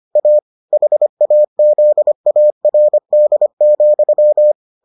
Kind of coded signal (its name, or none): Morse code